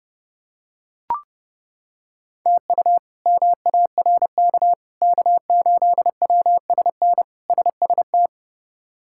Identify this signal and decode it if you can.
Morse code, heard as a keyed tone, about 30 wpm, 700 Hz. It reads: TU MARK K8WSN SST
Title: K1USN SST QSO Practice - 30wpm